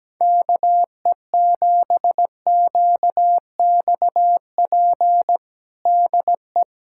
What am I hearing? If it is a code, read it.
Morse code, heard as a keyed tone, about 17 wpm, 700 Hz. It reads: KE7QXP DE
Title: USA Callsign DE Callsign 17wpm